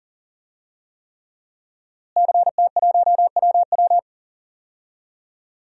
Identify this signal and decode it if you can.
Morse code, heard as a keyed tone, about 37 wpm, 700 Hz. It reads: CT1WW
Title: DXCC-100 Call Signs - 5 Characters - Repeated 3x - 40wpm